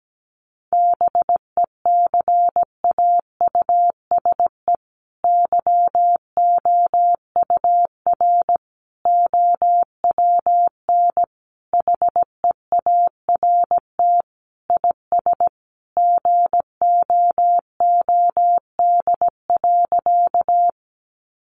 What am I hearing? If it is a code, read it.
Morse code, heard as a keyed tone, about 17 wpm, 700 Hz. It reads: BECAUSE YOUR OWN HEART IS GOOD.
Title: Sentences from Top 1200 Words 17wpm